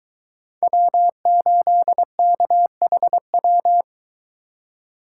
Morse code, heard as a keyed tone, about 23 wpm, 700 Hz. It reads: W8KHW